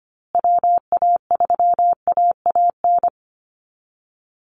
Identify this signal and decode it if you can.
Morse code, heard as a keyed tone, about 25 wpm, 700 Hz. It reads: WA3AAN
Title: Call Signs Repeated 2x - 25wpm